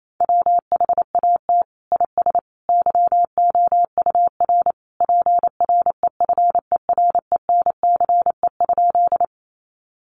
Morse code, heard as a keyed tone, about 28 wpm, 700 Hz. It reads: WHAT IS YOUR PREFERENCE?
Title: Sentences from Top 1500 Words 28wpm